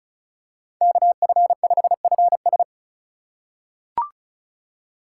Morse code, heard as a keyed tone, about 34 wpm, 700 Hz. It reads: KF5FS
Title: USA Call Signs - 5 Characters - 4x Character Spacing - 35wpm